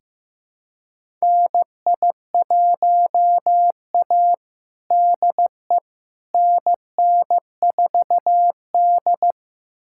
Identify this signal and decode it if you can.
Morse code, heard as a keyed tone, about 15 wpm, 700 Hz. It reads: NI1A DE NN4D
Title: USA Callsign DE Callsign - 4 Characters - 15wpm